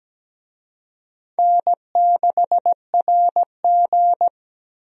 Morse code, heard as a keyed tone, about 17 wpm, 700 Hz. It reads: N6RG
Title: USA Call Signs - 4 Characters - Repeated 2x - 17wpm